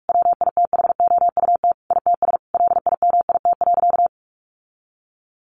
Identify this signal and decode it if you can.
Morse code, heard as a keyed tone, about 44 wpm, 700 Hz. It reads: WITHOUT ITS LIMIT.
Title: Sentences from Top 1500 Words 45wpm